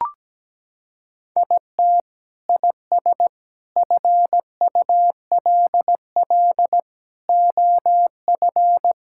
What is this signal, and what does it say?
Morse code, heard as a keyed tone, about 17 wpm, 700 Hz. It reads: IT IS FULL OF
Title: Sentences from Top 1000 Words - Review Part 5 - 17wpm